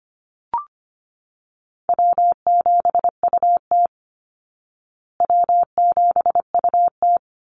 Morse code, heard as a keyed tone, about 25 wpm, 700 Hz. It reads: W7UT W7UT
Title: Call Signs Repeated 3x - 25wpm